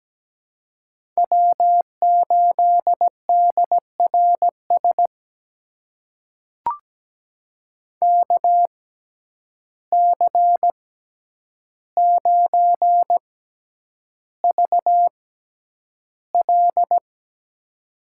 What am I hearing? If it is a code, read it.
Morse code, heard as a keyed tone, about 17 wpm, spaced as for 9 wpm, 700 Hz. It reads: W8DRS KC9VL
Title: USA Call Signs - 5 Characters - 6x Character Spacing - 17wpm